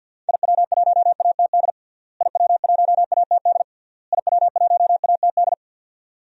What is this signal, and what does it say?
Morse code, heard as a keyed tone, about 53 wpm, 700 Hz. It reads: IW1ATD IW1ATD IW1ATD
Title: DXCC-100 Call Signs - 6 Characters - Repeated 3x - 50wpm